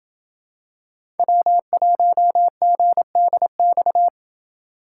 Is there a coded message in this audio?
Yes: Morse code from 1.2 to 4.1 s, heard as a keyed tone, about 27 wpm, 700 Hz. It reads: W1GDX